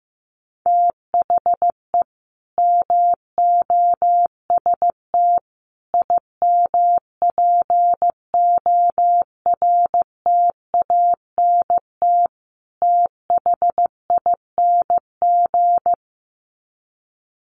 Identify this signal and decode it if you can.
Morse code, heard as a keyed tone, about 14 wpm, 700 Hz. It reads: THE MOST IMPORTANT THING
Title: Sets of 4 Words 15wpm